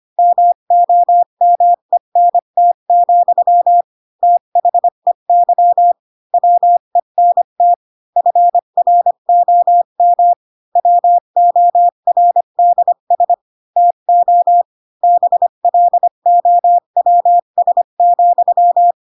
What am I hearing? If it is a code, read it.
Morse code, heard as a keyed tone, about 25 wpm, 700 Hz. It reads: MOMENT, THEY WENT FROM WORDS TO BLOWS,